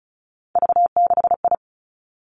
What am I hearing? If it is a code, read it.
Morse code, heard as a keyed tone, about 36 wpm, 700 Hz. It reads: V6I
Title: Three Challenging Letter-Number Combinations 35wpm